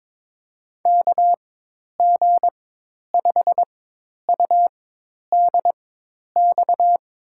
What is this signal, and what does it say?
Morse code, heard as a keyed tone, about 23 wpm, spaced as for 10 wpm, 700 Hz. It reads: KG5UDX
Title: USA Call Signs - 4x Character Spacing - 22wpm